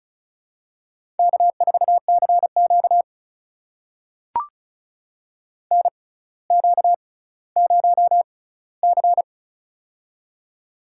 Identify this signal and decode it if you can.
Morse code, heard as a keyed tone, about 36 wpm, spaced as for 18 wpm, 700 Hz. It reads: K4CQ NQ0C
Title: USA Call Signs - 4 Characters - 6x Character Spacing - 35wpm